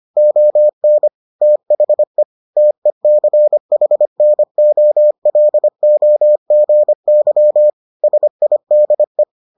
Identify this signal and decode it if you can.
Morse code, heard as a keyed tone, about 25 wpm, 600 Hz. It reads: ON THE TECHNOLOGY SIDE